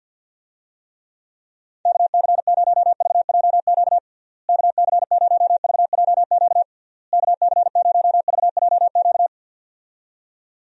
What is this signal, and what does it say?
Morse code, heard as a keyed tone, about 50 wpm, 700 Hz. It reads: KC0VJQ KC0VJQ KC0VJQ
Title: USA Call Signs - 6 Characters - Repeated 3x - 50wpm